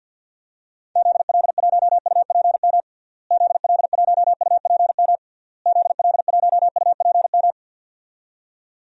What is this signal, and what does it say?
Morse code, heard as a keyed tone, about 48 wpm, 700 Hz. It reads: ZL1UPM ZL1UPM ZL1UPM
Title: DXCC-100 Call Signs - 6 Characters - Repeated 3x - 50wpm